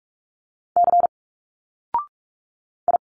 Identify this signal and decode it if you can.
Morse code, heard as a keyed tone, about 46 wpm, 700 Hz. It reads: C I
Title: Single Letters 45wpm